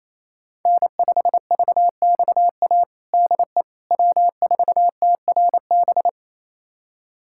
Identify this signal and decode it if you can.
Morse code, heard as a keyed tone, about 28 wpm, 700 Hz. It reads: N5VXA DE W4TRB
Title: USA Callsign DE Callsign - 5 Characters - 28wpm